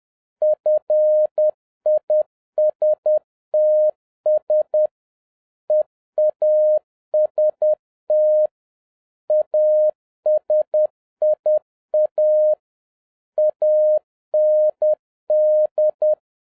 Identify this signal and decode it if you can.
Morse code, heard as a keyed tone, about 10 wpm, 610 Hz. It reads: FISTS EAST ASIA AND
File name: 2011-07-mp3 / n36 THE 3.11 2 10wpm